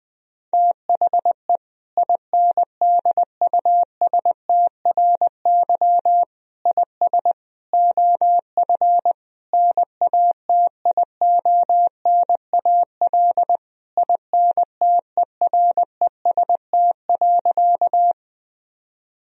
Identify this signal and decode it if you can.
Morse code, heard as a keyed tone, about 20 wpm, 700 Hz. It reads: THE INDUSTRY IS OF NATIONAL INTEREST.
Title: Sentences from Top 1000 Words - Review Part 4 - 20wpm